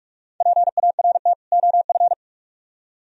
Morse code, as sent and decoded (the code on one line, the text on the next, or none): .--. .- .-. - / --- ..-.
PART OF